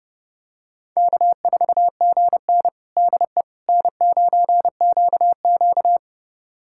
Morse code, heard as a keyed tone, about 31 wpm, 700 Hz. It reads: K4GN DE N9QQ
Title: USA Callsign DE Callsign - 4 Characters - 30wpm